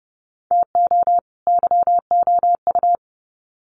Morse code, heard as a keyed tone, about 30 wpm, 700 Hz. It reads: TO YOU
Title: Sets of 2 Words 30wpm